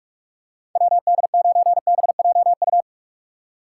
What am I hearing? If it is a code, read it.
Morse code, heard as a keyed tone, about 46 wpm, 700 Hz. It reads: WD9BJU
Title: USA Call Signs - 6 Characters - 6x Character Spacing - 45wpm